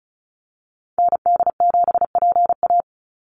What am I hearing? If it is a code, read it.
Morse code, heard as a keyed tone, about 34 wpm, 700 Hz. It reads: ND7PA
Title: NA SOTA Activator Callsigns 35wpm